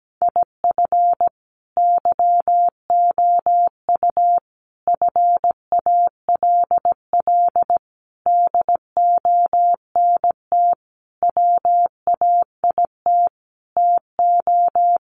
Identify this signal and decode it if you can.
Morse code, heard as a keyed tone, about 17 wpm, 700 Hz. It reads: IF YOU FALL DONT WAIT TO
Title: Sentences from Top 500 Words 17wpm